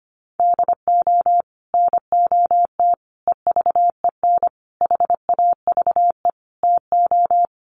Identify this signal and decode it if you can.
Morse code, heard as a keyed tone, about 25 wpm, 700 Hz. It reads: DO NOT EVEN HAVE TO
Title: Sets of 5 Words Encore2 25wpm